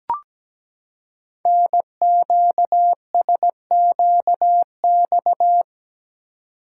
Morse code, heard as a keyed tone, about 17 wpm, 700 Hz. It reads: NQSQX
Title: Code Groups - 5 Letters - 17wpm